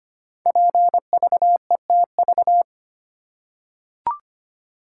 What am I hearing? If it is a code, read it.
Morse code, heard as a keyed tone, about 25 wpm, 700 Hz. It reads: PVETV
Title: Code Groups - 5 Letters - 25wpm